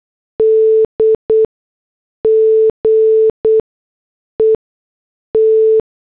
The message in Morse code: DGET